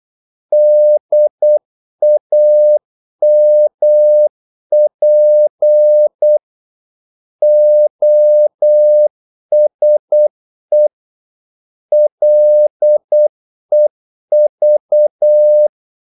Morse code, heard as a keyed tone, about 8 wpm, 600 Hz. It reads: DAMP OSE LEV
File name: Korte ord DK 8wpm